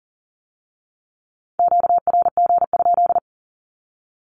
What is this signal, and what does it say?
Morse code, heard as a keyed tone, about 38 wpm, 700 Hz. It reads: QRG?
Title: Q-Codes with Spoken Meaning - 40wpm